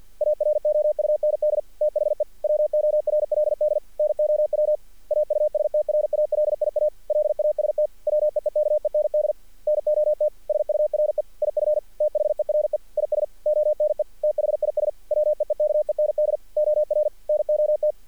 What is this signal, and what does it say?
Morse code, heard as a keyed tone, about 48 wpm, 600 Hz. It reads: AROUND THE WORLD NOW AUSTRALIA LAST WEEKEND NOT SURE IF THERE IS ONE THIS WEEKEND OR NOT
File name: archived / CRSnet